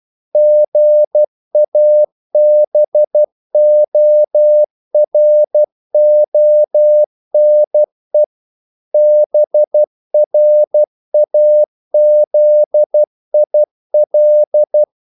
Morse code, heard as a keyed tone, about 12 wpm, 600 Hz. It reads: GABORONE BRAZIL
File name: CW / Practice / NationalCapitals12